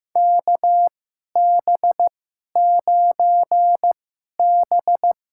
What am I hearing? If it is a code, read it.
Morse code, heard as a keyed tone, about 15 wpm, spaced as for 11 wpm, 700 Hz. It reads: KB9B
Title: USA Call Signs - 2x Character Spacing - 15wpm